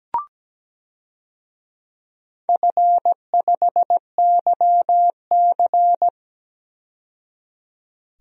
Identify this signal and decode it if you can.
Morse code, heard as a keyed tone, about 17 wpm, 700 Hz. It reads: F5YC
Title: DXCC-100 Call Signs - 4 Characters - 17wpm